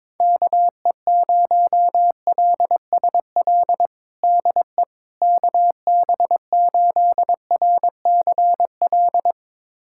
Morse code, heard as a keyed tone, about 22 wpm, 700 Hz. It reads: KE0LSL DE KB8RCL
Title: USA Callsign DE Callsign 22wpm